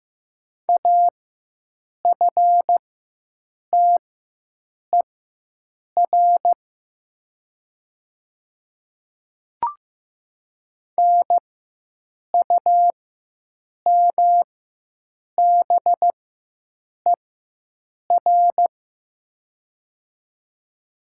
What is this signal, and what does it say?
Morse code, heard as a keyed tone, about 15 wpm, spaced as for 7 wpm, 700 Hz. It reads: AFTER NUMBER
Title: 4x Character Spacing - Top 200 Words - 15wpm